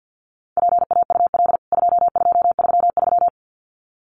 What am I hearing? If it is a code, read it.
Morse code, heard as a keyed tone, about 50 wpm, 700 Hz. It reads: PAUL 2233
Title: CWT Exchange 50wpm